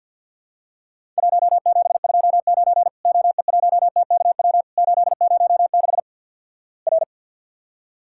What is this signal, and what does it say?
Morse code, heard as a keyed tone, about 52 wpm, 700 Hz. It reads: 1729 OE1TKW 806 R
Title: Worked All Europe DX Contest - QTC Exchange with Repeats - 50wpm